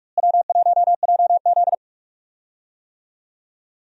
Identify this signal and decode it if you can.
Morse code, heard as a keyed tone, about 46 wpm, 700 Hz. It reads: W1JZ
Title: USA Call Signs - 4 Characters - Repeated 3x - 45wpm